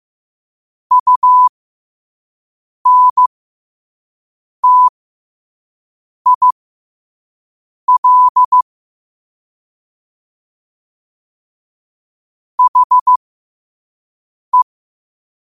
Morse code, heard as a keyed tone, about 15 wpm, spaced as for 5 wpm, 1000 Hz. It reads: UNTIL HE
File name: qotdfiles / 5wpm